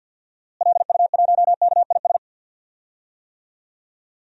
Morse code, heard as a keyed tone, about 52 wpm, 700 Hz. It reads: LU1KIS
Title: DXCC-100 Call Signs - 6 Characters - Repeated 2x - 50wpm